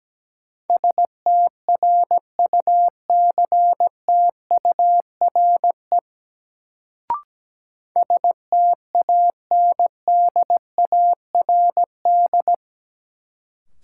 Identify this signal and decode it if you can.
Morse code, heard as a keyed tone, about 17 wpm, 700 Hz. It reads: STRUCTURE STANDARD
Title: Sentences from Top 1000 Words - Review Part 6 - New Word Drill - 17wpm